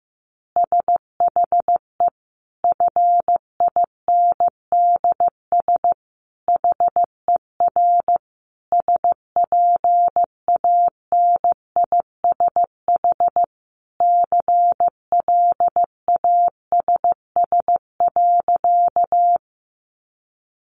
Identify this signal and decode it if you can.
Morse code, heard as a keyed tone, about 15 wpm, 700 Hz. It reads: SHE FINDS HER SPANISH CLASS.
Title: Additional Easy Phrases 15wpm